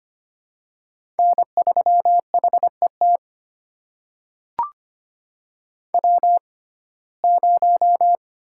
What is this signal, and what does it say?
Morse code, heard as a keyed tone, about 25 wpm, spaced as for 18 wpm, 700 Hz. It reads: N3HET W0
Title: USA Call Signs - 5 Characters - 6x Character Spacing - 25wpm